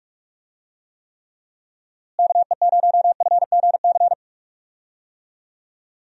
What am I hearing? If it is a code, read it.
Morse code, heard as a keyed tone, about 44 wpm, 700 Hz. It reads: KE0FGC
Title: USA Call Signs - 6 Characters - Repeated 3x - 45wpm